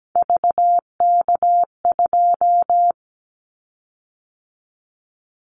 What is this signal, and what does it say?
Morse code, heard as a keyed tone, about 17 wpm, 700 Hz. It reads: VK2
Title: Callsign Prefixes for Top 50 DXCC - 17wpm